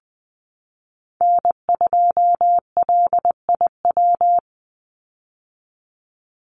Morse code, heard as a keyed tone, about 20 wpm, 700 Hz. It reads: N2LIW
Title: USA Call Signs 20wpm